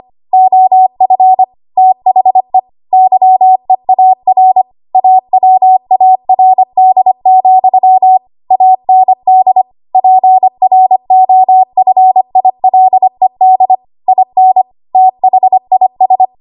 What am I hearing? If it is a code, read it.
Morse code, heard as a keyed tone, about 25 wpm, 750 Hz. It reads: OF THE YEAR AWARD, AND PROFILED IN THIS
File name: Archive / 25 WPM / 25WPM